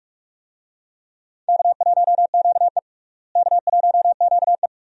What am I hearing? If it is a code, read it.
Morse code, heard as a keyed tone, about 46 wpm, 700 Hz. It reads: K1QE K1QE
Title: USA Call Signs - 4 Characters - Repeated 2x - 45wpm